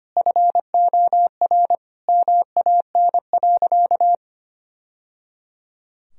Morse code, heard as a keyed tone, about 25 wpm, 700 Hz. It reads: FOR MAN.
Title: Sentences from Top 1000 Words - Review Part 3 - 25wpm